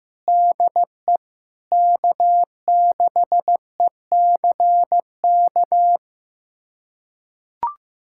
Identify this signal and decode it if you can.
Morse code, heard as a keyed tone, about 15 wpm, 700 Hz. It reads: DE K6ECK
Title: USA Callsign DE Callsign - 5 Characters - 15wpm